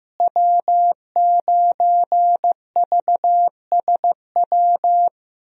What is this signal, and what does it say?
Morse code, heard as a keyed tone, about 15 wpm, 700 Hz. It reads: W9VSW
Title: USA Call Signs - 5 Characters - Repeated 3x - 15wpm